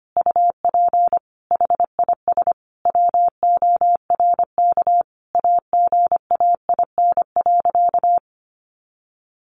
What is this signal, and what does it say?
Morse code, heard as a keyed tone, about 25 wpm, 700 Hz. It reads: UP HIS WORK AGAIN.
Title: Sentences from Top 1500 Words 25wpm